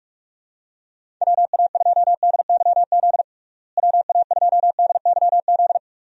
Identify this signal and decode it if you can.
Morse code, heard as a keyed tone, about 46 wpm, 700 Hz. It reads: WA2DYZ WA2DYZ
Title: USA Call Signs - 6 Characters - Repeated 3x - 45wpm